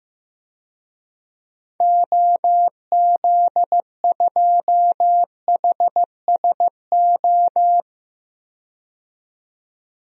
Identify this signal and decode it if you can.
Morse code, heard as a keyed tone, about 15 wpm, 700 Hz. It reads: OZ2HSO
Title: DXCC-100 Call Signs - 6 Characters - 15wpm